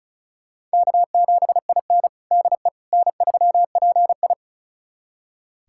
Morse code, heard as a keyed tone, about 36 wpm, 700 Hz. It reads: K7IN DE N3PI
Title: USA Callsign DE Callsign - 4 Characters - 35wpm